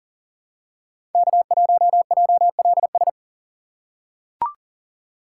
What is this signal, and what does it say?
Morse code, heard as a keyed tone, about 40 wpm, 700 Hz. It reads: K1JLS
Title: USA Call Signs - 5 Characters - 2x Character Spacing - 40wpm